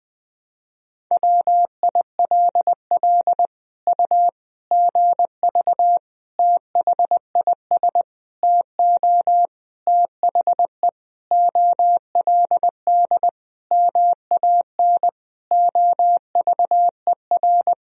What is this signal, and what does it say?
Morse code, heard as a keyed tone, about 20 wpm, 700 Hz. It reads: WILL U GV THIS TO THE OLD MAN OVER
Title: Sentences from Top 200 Words and Abbreviations 20wpm